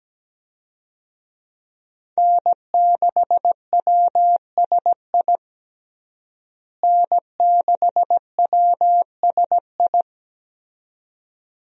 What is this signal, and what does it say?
Morse code, heard as a keyed tone, about 17 wpm, 700 Hz. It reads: N6WSI N6WSI
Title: USA Call Signs - 5 Characters - Repeated 2x - 17wpm